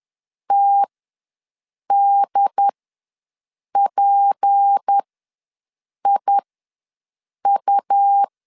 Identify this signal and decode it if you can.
Morse code, heard as a keyed tone, about 10 wpm, spaced as for 6 wpm, 800 Hz. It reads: TDPIU
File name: LEZIONI / 6 PKX0 / 6 P ws 10wpm 3751